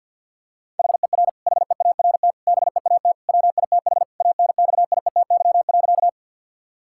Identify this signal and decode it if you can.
Morse code, heard as a keyed tone, about 50 wpm, 700 Hz. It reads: HER HEART BEAT WITH ANXIETY.
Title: Sentences from Top 1600 Words 50wpm